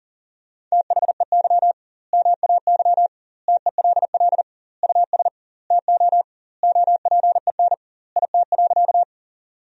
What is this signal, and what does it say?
Morse code, heard as a keyed tone, about 40 wpm, 700 Hz. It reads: THEY MAY TELL US TO OPEN IT.
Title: 2x Word-Spacing - Sentences from Top 300 Words - 40wpm